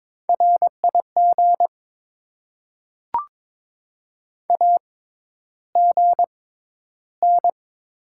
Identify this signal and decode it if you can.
Morse code, heard as a keyed tone, about 22 wpm, spaced as for 11 wpm, 700 Hz. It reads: RIG AGN
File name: Top 20 QSO Elements - 6x Character Spacing - 22wpm